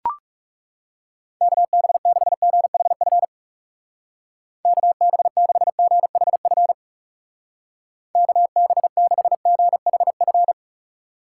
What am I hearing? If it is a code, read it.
Morse code, heard as a keyed tone, about 37 wpm, 700 Hz. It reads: KB6GHF KB6GHF KB6GHF
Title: Speed Racing - USA 2x3 Call Signs 30wpm